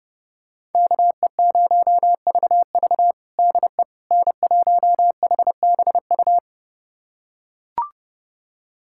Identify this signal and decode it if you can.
Morse code, heard as a keyed tone, about 30 wpm, 700 Hz. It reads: KE0VV DE N1HBU
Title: USA Callsign DE Callsign - 5 Characters - 30wpm